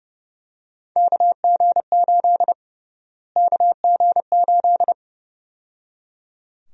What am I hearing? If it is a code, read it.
Morse code, heard as a keyed tone, about 30 wpm, 700 Hz. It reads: KG8 KG8
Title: USA Call Sign Prefixes - Repeated 2x - 30wpm